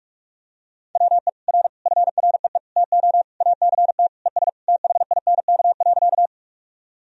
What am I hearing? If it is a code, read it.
Morse code, heard as a keyed tone, about 46 wpm, 700 Hz. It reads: WE R FREE TO ACT ES THINK.
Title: Sentences from Top 500 Words and Abbreviations 45wpm